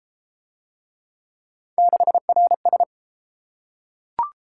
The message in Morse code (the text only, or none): 6RS